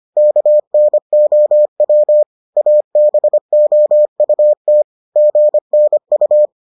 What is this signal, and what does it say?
Morse code, heard as a keyed tone, about 25 wpm, 600 Hz. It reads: KNOW ABOUT GNU